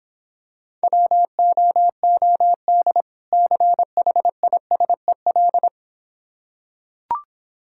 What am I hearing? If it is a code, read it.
Morse code, heard as a keyed tone, about 26 wpm, 700 Hz. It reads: WOOD CHISEL